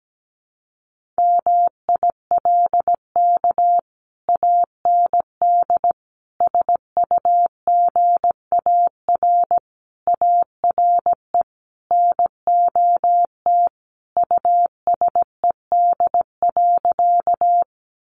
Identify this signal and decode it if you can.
Morse code, heard as a keyed tone, about 17 wpm, 700 Hz. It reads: MILK AND SUGAR ARE NOT USED.
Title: Sentences from Top 1500 Words 17wpm